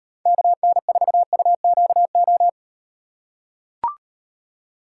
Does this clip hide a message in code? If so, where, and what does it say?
Yes: Morse code from 0.3 to 2.5 s, heard as a keyed tone, about 38 wpm, 700 Hz. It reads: KN4UQO